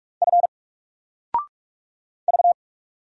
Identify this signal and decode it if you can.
Morse code, heard as a keyed tone, about 44 wpm, 700 Hz. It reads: F V
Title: Single Letter-Number 45wpm